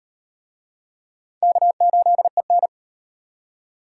Morse code, heard as a keyed tone, about 36 wpm, 700 Hz. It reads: K8EN